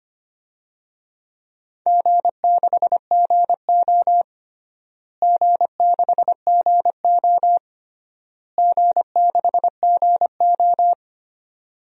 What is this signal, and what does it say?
Morse code, heard as a keyed tone, about 25 wpm, 700 Hz. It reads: G6GO G6GO G6GO
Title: DXCC-100 Call Signs - 4 Characters - Repeated 3x - 25wpm